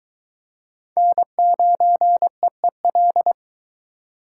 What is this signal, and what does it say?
Morse code, heard as a keyed tone, about 22 wpm, 700 Hz. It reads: N9EEL